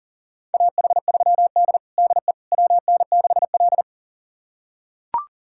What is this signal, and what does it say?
Morse code, heard as a keyed tone, about 40 wpm, 700 Hz. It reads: AH3D DE WN6L
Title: USA Callsign DE Callsign - 4 Characters - 40wpm